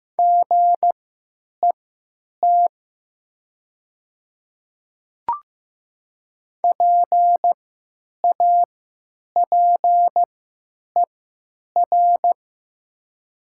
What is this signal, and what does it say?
Morse code, heard as a keyed tone, about 15 wpm, spaced as for 9 wpm, 700 Hz. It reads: GET PAPER
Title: 3x Character Spacing - Top 300 Words - 15wpm